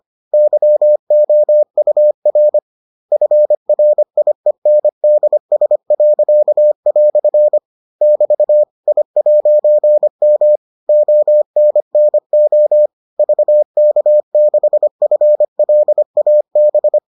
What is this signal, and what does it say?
Morse code, heard as a keyed tone, about 25 wpm, 600 Hz. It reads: YOUR FRIENDS." <BT> I'M ONNO VK6FLAB